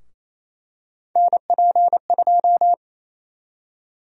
Morse code, heard as a keyed tone, about 27 wpm, 700 Hz. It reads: NP2
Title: USA Call Sign Prefixes - Repeated 2x - 28wpm